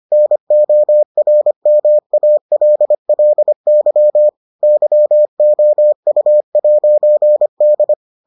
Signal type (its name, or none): Morse code